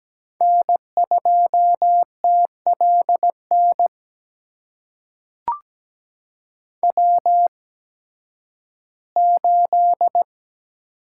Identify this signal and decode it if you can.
Morse code, heard as a keyed tone, about 17 wpm, spaced as for 11 wpm, 700 Hz. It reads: N2TLN W8
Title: USA Call Signs - 5 Characters - 8x Character Spacing - 17wpm